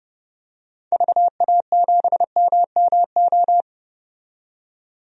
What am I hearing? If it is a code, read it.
Morse code, heard as a keyed tone, about 32 wpm, 700 Hz. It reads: VA7MMO
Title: DXCC-100 Call Signs - 6 Characters - 30wpm